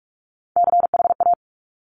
Morse code, heard as a keyed tone, about 46 wpm, 700 Hz. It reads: CHA
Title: Top Three Letter Combinations in Words 45wpm